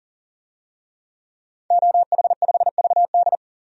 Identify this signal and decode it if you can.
Morse code, heard as a keyed tone, about 40 wpm, 700 Hz. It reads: OH5VD
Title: DXCC-100 Call Signs - 5 Characters - 40wpm